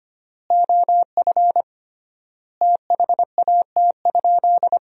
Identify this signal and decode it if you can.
Morse code, heard as a keyed tone, about 25 wpm, 700 Hz. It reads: OF THAT?
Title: 3x Word-Spacing - Sentences from Top 100 Words - 25wpm